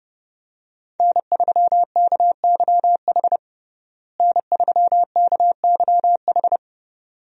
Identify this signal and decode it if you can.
Morse code, heard as a keyed tone, about 32 wpm, 700 Hz. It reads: N3KYH N3KYH
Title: USA Call Signs - 5 Characters - Repeated 2x - 30wpm